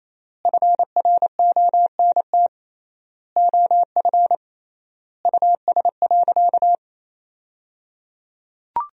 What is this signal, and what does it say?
Morse code, heard as a keyed tone, about 28 wpm, 700 Hz. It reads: FRONT OF US.
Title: 3x Word-Spacing - Sentences from Top 500 Words - 28wpm